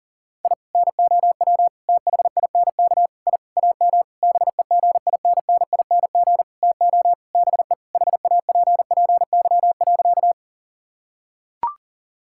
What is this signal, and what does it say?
Morse code, heard as a keyed tone, about 40 wpm, 700 Hz. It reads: I NOW THINK I AM BEGINNING TO BE HAPPY.
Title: Sentences from Top 1000 Words - Review Part 5 - 40wpm